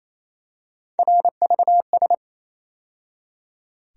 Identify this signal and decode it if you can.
Morse code, heard as a keyed tone, about 28 wpm, 700 Hz. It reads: RVS
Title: USA Call Sign Postfixes - 3 Letters 28wpm